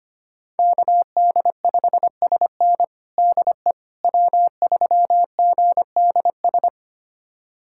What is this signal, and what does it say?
Morse code, heard as a keyed tone, about 25 wpm, 700 Hz. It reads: KD5SN DE W3GDS
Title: USA Callsign DE Callsign 25wpm